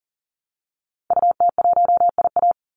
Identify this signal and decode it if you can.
Morse code, heard as a keyed tone, about 40 wpm, 700 Hz. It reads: UT1IA
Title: Call Signs - 40wpm